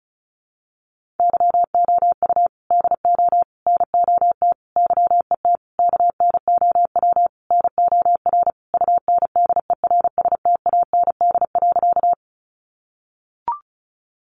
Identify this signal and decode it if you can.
Morse code, heard as a keyed tone, about 34 wpm, 700 Hz. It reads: YOU DO NOT YET KNOW NOR UNDERSTAND.
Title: Sentences from Top 1500 Words 35wpm